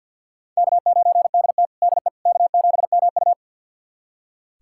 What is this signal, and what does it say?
Morse code, heard as a keyed tone, about 46 wpm, 700 Hz. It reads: K9DT DE K7MU
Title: USA Callsign DE Callsign - 4 Characters - 50wpm